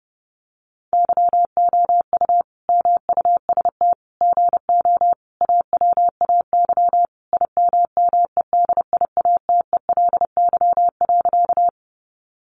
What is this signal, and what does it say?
Morse code, heard as a keyed tone, about 30 wpm, 700 Hz. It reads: YOU MUST GO AWAY IMMEDIATELY.
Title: Sentences from Top 1100 Words 30wpm